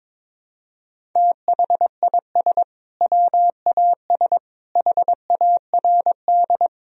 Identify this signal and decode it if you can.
Morse code, heard as a keyed tone, about 22 wpm, 700 Hz. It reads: THIS WAS HARD
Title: Sentences from Top 1000 Words - Review Part 1 - 22wpm